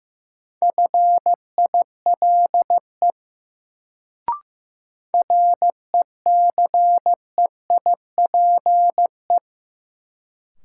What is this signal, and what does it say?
Morse code, heard as a keyed tone, about 16 wpm, 700 Hz. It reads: FILE RECEIPE
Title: Sentences from Top 1200 Words - New Word Drill Alt - 15wpm